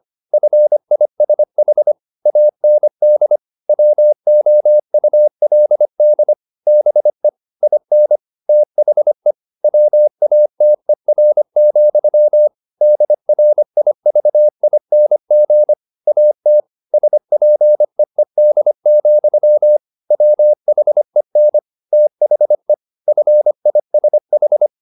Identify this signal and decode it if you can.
Morse code, heard as a keyed tone, about 25 wpm, 600 Hz. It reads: FISH AND WOULD BE IN THE WATER, DRIVING AT SPEED, WHEN THE FISH
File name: the old man and the sea part 1 25wpm